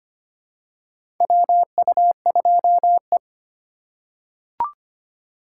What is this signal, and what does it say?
Morse code, heard as a keyed tone, about 25 wpm, 700 Hz. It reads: WU2E
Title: USA Call Signs - 4 Characters - 3x Character Spacing - 25wpm